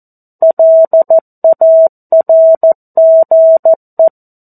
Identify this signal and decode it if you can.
Morse code, heard as a keyed tone, about 14 wpm, 650 Hz. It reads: LARGE
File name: 02-Chapter-02 14 WPM